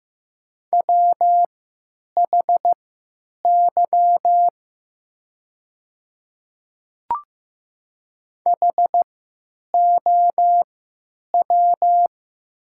Morse code, heard as a keyed tone, about 15 wpm, spaced as for 9 wpm, 700 Hz. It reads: WHY HOW
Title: 3x Character Spacing - Top 300 Words - 15wpm